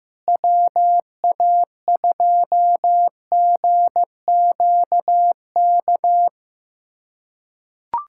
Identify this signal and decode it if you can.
Morse code, heard as a keyed tone, about 15 wpm, 700 Hz. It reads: WA2GQK
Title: USA Callsign DE Callsign - 6 Characters - 15wpm